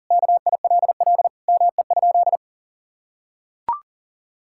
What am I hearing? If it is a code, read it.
Morse code, heard as a keyed tone, about 40 wpm, 700 Hz. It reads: KILL ME?
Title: Sentences from Top 1600 Words 40wpm